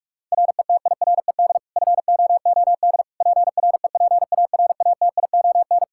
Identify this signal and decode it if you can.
Morse code, heard as a keyed tone, about 45 wpm, 700 Hz. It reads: RETIRED FOOD PREPARATION
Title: Practice QSO - Exchange 3 - More Info - 45wpm